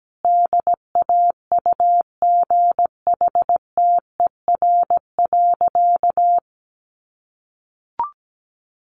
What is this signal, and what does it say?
Morse code, heard as a keyed tone, about 17 wpm, 700 Hz. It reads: DAUGHTER.
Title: Sentences from Top 1500 Words 17wpm